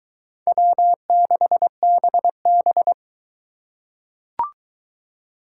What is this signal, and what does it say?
Morse code, heard as a keyed tone, about 22 wpm, 700 Hz. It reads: W6BB